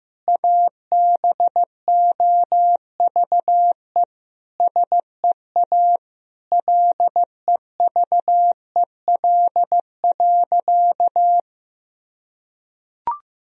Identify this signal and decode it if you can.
Morse code, heard as a keyed tone, about 15 wpm, 700 Hz. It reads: ABOVE SEA LEVEL.
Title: Sentences from Top 1000 Words - Review Part 4 - 15wpm